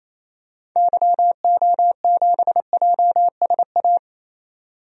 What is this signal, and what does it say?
Morse code, heard as a keyed tone, about 28 wpm, 700 Hz. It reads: YO7JSA
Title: DXCC-100 Call Signs - 6 Characters - 28wpm